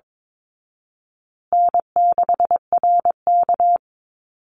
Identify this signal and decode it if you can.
Morse code, heard as a keyed tone, about 22 wpm, 700 Hz. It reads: N6RK
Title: Easy Call Signs Repeated 2x - 22wpm